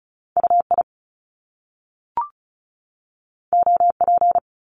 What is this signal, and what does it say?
Morse code, heard as a keyed tone, about 36 wpm, 700 Hz. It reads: UI OP
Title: Top Two Letter Combinations in Words 35wpm